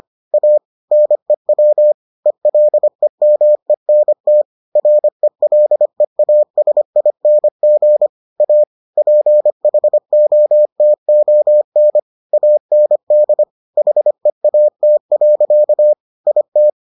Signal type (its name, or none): Morse code